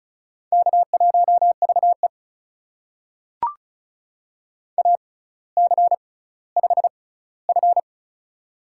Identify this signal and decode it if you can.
Morse code, heard as a keyed tone, about 36 wpm, spaced as for 18 wpm, 700 Hz. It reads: K1VE AC5F
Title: USA Call Signs - 4 Characters - 6x Character Spacing - 35wpm